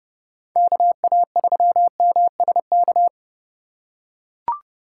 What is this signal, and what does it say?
Morse code, heard as a keyed tone, about 28 wpm, 700 Hz. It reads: KA3MSK